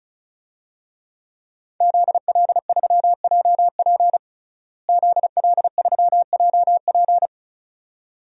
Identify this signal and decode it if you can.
Morse code, heard as a keyed tone, about 34 wpm, 700 Hz. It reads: ZL3JP ZL3JP
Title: DXCC-100 Call Signs - 5 Characters - Repeated 2x - 35wpm